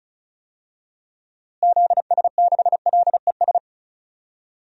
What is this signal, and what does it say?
Morse code, heard as a keyed tone, about 36 wpm, 700 Hz. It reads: ZS6LES
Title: DXCC-100 Call Signs - 6 Characters - Repeated 3x - 35wpm